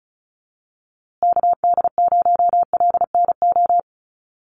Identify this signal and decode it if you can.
Morse code, heard as a keyed tone, about 34 wpm, 700 Hz. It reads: KD0LNO
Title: USA Call Signs Repeated 3x 35wpm